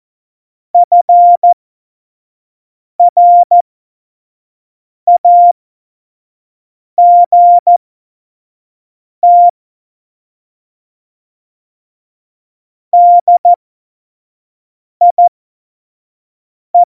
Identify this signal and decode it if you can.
Morse code, heard as a keyed tone, about 14 wpm, spaced as for 5 wpm, 700 Hz. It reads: FRAGT DIE
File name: Morsekurs / Witze / Witz 03.Juli 2018, 25 F70